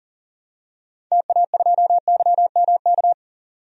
Morse code, heard as a keyed tone, about 37 wpm, 700 Hz. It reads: TA2YMK
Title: DXCC-100 Call Signs - 6 Characters - 40wpm